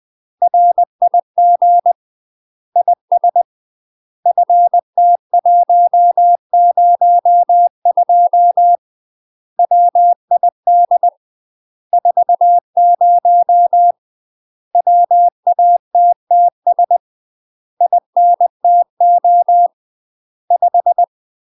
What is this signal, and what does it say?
Morse code, heard as a keyed tone, about 20 wpm, 700 Hz. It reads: RIG IS FT102 WID 40 WATTS INTO 5
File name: UK MASTER QSOs TRAINING / 20WPM / UK MASTER QSOs 0068